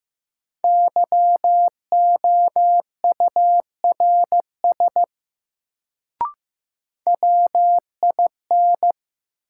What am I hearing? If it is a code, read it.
Morse code, heard as a keyed tone, about 15 wpm, 700 Hz. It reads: YOURS WIN
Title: Sentences from Top 1500 Words - New Word Drill Alt - 15wpm